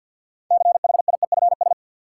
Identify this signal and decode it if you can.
Morse code, heard as a keyed tone, about 51 wpm, 700 Hz. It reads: CHIEFS
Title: Sets of 5 Words Encore - New Word Drill - 50wpm